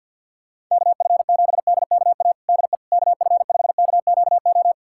Morse code, heard as a keyed tone, about 52 wpm, 700 Hz. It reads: KF7DKA DE KF5KQO
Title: USA Callsign DE Callsign - 6 Characters - 50wpm